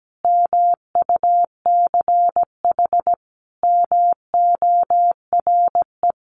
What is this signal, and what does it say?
Morse code, heard as a keyed tone, about 17 wpm, 700 Hz. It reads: MUCH MORE
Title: Sentences from Top 1100 Words 17wpm